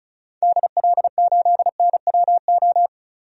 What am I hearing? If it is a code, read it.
Morse code, heard as a keyed tone, about 36 wpm, 700 Hz. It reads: DL8NWO
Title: DXCC-100 Call Signs - 6 Characters - 35wpm